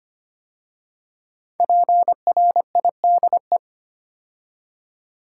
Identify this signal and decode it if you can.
Morse code, heard as a keyed tone, about 25 wpm, 700 Hz. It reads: PRIDE
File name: 5 Letter Words - First 3 Letters Given - 25wpm